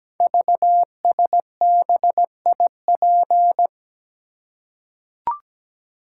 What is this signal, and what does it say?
Morse code, heard as a keyed tone, about 17 wpm, 700 Hz. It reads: VSBIP
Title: Code Groups - 5 Letter:Numbers - 17wpm